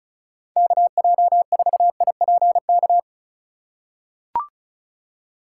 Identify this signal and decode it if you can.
Morse code, heard as a keyed tone, about 36 wpm, 700 Hz. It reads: KJ4IPK
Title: USA Call Signs - 6 Characters - 2x Character Spacing - 35wpm